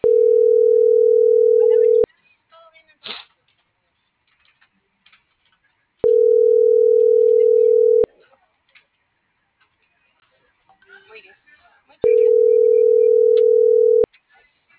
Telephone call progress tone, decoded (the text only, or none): ringback tone